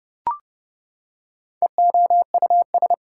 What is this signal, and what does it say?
Morse code, heard as a keyed tone, about 30 wpm, 700 Hz. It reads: EOUS
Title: Prefixes and Suffixes 30wpm